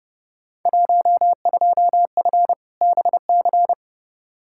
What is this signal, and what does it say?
Morse code, heard as a keyed tone, about 30 wpm, 700 Hz. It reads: 12F BC
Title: Field Day Exchanges Encore - 30wpm